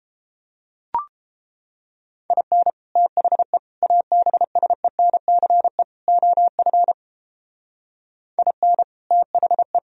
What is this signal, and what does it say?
Morse code, heard as a keyed tone, about 32 wpm, 700 Hz. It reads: IN THE ABSENCE OF IN THE